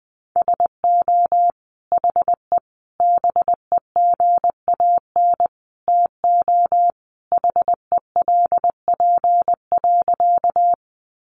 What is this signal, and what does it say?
Morse code, heard as a keyed tone, about 20 wpm, 700 Hz. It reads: SO HE BEGAN TO HELP.
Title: Sentences from Top 300 Words 20wpm